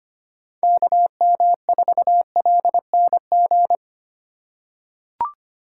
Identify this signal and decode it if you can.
Morse code, heard as a keyed tone, about 25 wpm, 700 Hz. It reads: KM4LNG
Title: USA Call Signs - 6 Characters - 2x Character Spacing - 25wpm